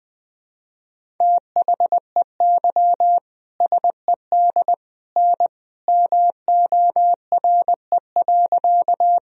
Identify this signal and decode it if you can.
Morse code, heard as a keyed tone, about 20 wpm, 700 Hz. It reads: THEY SED N MORE.
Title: Sentences from Top 100 Words and Abbreviations 20wpm